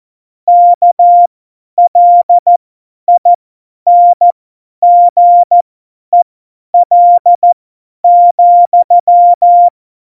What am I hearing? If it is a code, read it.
Morse code, heard as a keyed tone, about 14 wpm, spaced as for 10 wpm, 700 Hz. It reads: KLINGEL,
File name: Morsekurs / Witze / Witz 05. Februar 2019, 50 F70